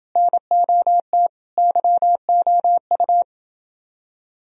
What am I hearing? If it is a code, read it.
Morse code, heard as a keyed tone, about 29 wpm, 700 Hz. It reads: NOT YOU